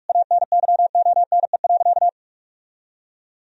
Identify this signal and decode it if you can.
Morse code, heard as a keyed tone, about 46 wpm, 700 Hz. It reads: ANYONE.
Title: Sentences from Top 1000 Words - Review Part 2 - 45wpm